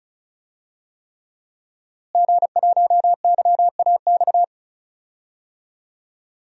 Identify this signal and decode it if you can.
Morse code, heard as a keyed tone, about 34 wpm, 700 Hz. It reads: G1YAX
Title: DXCC-100 Call Signs - 5 Characters - Repeated 3x - 35wpm